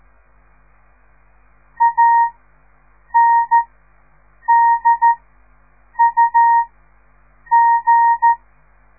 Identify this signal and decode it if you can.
Morse code, heard as a keyed tone, about 13 wpm, spaced as for 7 wpm, 930 Hz. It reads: ANDUG
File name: radiomorningstar / morse / les 4